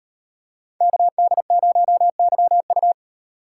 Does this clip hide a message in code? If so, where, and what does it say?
Yes: Morse code from 0.8 to 2.9 s, heard as a keyed tone, about 38 wpm, 700 Hz. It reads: KD0YU